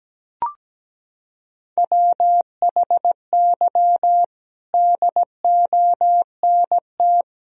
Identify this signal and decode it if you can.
Morse code, heard as a keyed tone, about 17 wpm, 700 Hz. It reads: WHY DONT
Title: Sentences from Top 300 Words and Abbreviations 17wpm